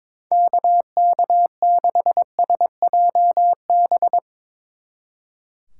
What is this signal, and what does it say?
Morse code, heard as a keyed tone, about 22 wpm, 700 Hz. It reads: KK6SJB
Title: Speed Racing - USA 2x3 Call Signs 22wpm